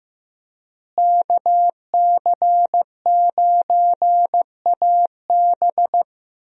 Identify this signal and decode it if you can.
Morse code, heard as a keyed tone, about 16 wpm, 700 Hz. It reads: KC9AB
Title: USA Call Signs - 5 Characters - Repeated 3x - 15wpm